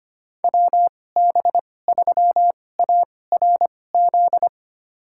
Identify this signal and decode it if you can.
Morse code, heard as a keyed tone, about 25 wpm, spaced as for 18 wpm, 700 Hz. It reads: WB3ARZ
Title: USA Call Signs - 6 Characters - 2x Character Spacing - 25wpm